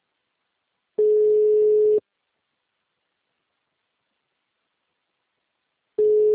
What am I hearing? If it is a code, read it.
Ringback tone (Europe, 425 Hz).